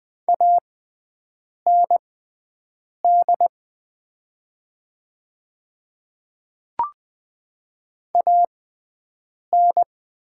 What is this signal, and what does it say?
Morse code, heard as a keyed tone, about 20 wpm, spaced as for 7 wpm, 700 Hz. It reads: AND AN
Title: 6x Character Spacing - Top 100 Words - 20wpm